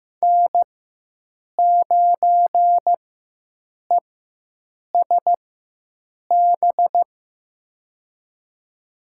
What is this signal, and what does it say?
Morse code, heard as a keyed tone, about 15 wpm, spaced as for 7 wpm, 700 Hz. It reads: N9ESB
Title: USA Call Signs - 5 Characters - 4x Character Spacing - 15wpm